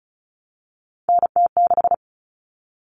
Morse code, heard as a keyed tone, about 34 wpm, 700 Hz. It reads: NT6
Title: All Cut Numbers - 3-Digit 35wpm